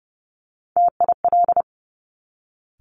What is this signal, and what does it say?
Morse code, heard as a keyed tone, about 30 wpm, 700 Hz. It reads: TIL
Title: Top Three Letter Combinations in Words 30wpm